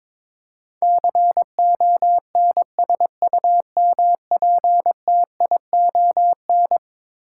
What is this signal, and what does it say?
Morse code, heard as a keyed tone, about 22 wpm, 700 Hz. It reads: CONSUMPTION
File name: practice / 22wpm / Eleven Letter Words - 22wpm